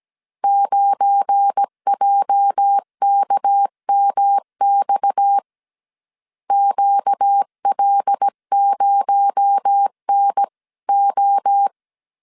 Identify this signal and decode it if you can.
Morse code, heard as a keyed tone, about 17 wpm, 800 Hz. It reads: 9JKMX QL0NO
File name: CW / DET / 1 9 1500C S A 16wpm 2828